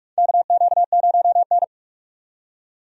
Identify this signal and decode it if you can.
Morse code, heard as a keyed tone, about 44 wpm, 700 Hz. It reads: KQ0N
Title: USA Call Signs - 4 Characters - 45wpm